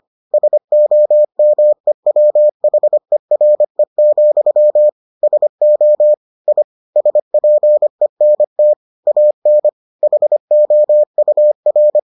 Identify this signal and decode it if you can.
Morse code, heard as a keyed tone, about 25 wpm, 600 Hz. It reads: SOMEWHERE, SO I SPENT AN HOUR